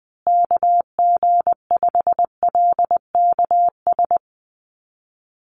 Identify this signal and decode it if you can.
Morse code, heard as a keyed tone, about 20 wpm, 700 Hz. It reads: KG5LKS
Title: USA Call Signs Repeated 3x 20wpm